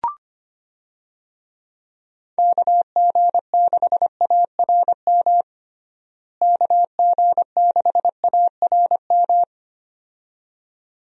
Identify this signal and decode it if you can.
Morse code, heard as a keyed tone, about 25 wpm, 700 Hz. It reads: KG6ARM KG6ARM
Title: USA Call Signs - 6 Characters - Repeated 2x - 25wpm